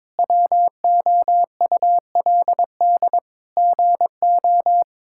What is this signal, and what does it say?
Morse code, heard as a keyed tone, about 22 wpm, 700 Hz. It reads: WOULD GO
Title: Sentences from Top 1000 Words - Review Part 6 - 22wpm